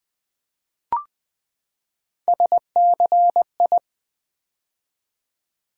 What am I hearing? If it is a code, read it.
Morse code, heard as a keyed tone, about 20 wpm, 700 Hz. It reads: SCI
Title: Build A Word - 3 Letters At A Time - 9-Letter Words - 20wpm